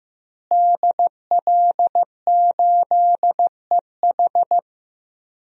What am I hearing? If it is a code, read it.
Morse code, heard as a keyed tone, about 15 wpm, 700 Hz. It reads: DL8EH
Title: DXCC-100 Call Signs - 5 Characters - Repeated 2x - 15wpm